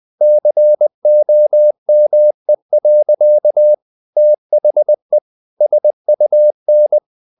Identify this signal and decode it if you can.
Morse code, heard as a keyed tone, about 20 wpm, 600 Hz. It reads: COME. THE SUN
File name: the old man and the sea part 2 20wpm